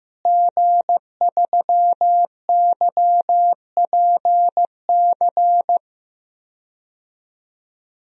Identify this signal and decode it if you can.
Morse code, heard as a keyed tone, about 15 wpm, 700 Hz. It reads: G3YPC
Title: DXCC-100 Call Signs - 5 Characters - Repeated 2x - 15wpm